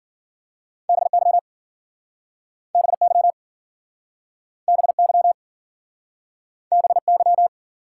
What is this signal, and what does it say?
Morse code, heard as a keyed tone, about 53 wpm, 700 Hz. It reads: BY BY BY BY